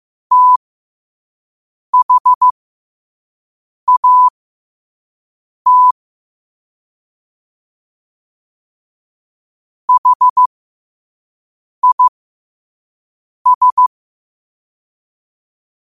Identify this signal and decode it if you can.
Morse code, heard as a keyed tone, about 15 wpm, spaced as for 5 wpm, 1000 Hz. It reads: THAT HIS
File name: qotdfiles / 5wpm